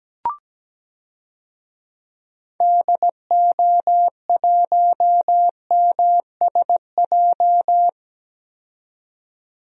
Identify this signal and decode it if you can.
Morse code, heard as a keyed tone, about 17 wpm, 700 Hz. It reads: DO1MSJ
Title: DXCC-100 Call Signs - 6 Characters - 17wpm